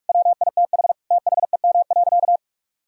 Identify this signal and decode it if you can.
Morse code, heard as a keyed tone, about 46 wpm, 700 Hz. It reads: WITH THEM.
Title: Sentences from Top 1000 Words - Review Part 6 - 45wpm